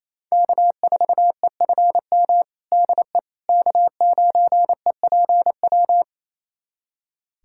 Morse code, heard as a keyed tone, about 27 wpm, 700 Hz. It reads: K4EFM DE K9EPW
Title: USA Callsign DE Callsign - 5 Characters - 28wpm